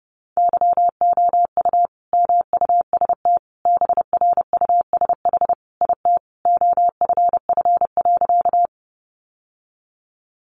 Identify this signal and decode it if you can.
Morse code, heard as a keyed tone, about 30 wpm, 700 Hz. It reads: YOU MUST BRUSH IT OFF.
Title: Sentences from Top 1100 Words 30wpm